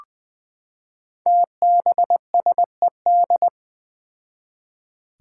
Morse code, heard as a keyed tone, about 20 wpm, 700 Hz. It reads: TBSED
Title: Five Challenging Letter-Number Combinations 20wpm